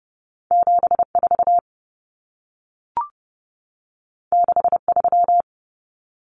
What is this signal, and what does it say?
Morse code, heard as a keyed tone, about 30 wpm, 700 Hz. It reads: 74 63
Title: Common Cut Numbers - 2-Digit 30wpm